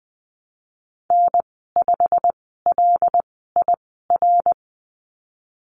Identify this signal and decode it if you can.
Morse code, heard as a keyed tone, about 20 wpm, spaced as for 15 wpm, 700 Hz. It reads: N5LIR
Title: 140 Lesson 23g - Letter f - Call Signs 20wpm